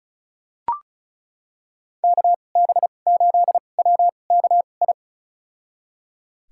Morse code, heard as a keyed tone, about 36 wpm, spaced as for 26 wpm, 690 Hz. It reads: KB8WKI
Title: USA Call Signs - 6 Characters - 2x Character Spacing - 35wpm